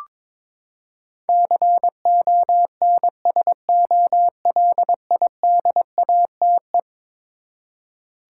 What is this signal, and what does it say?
Morse code, heard as a keyed tone, about 22 wpm, 700 Hz. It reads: CONSOLIDATE
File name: practice / 22wpm / Eleven Letter Words - 22wpm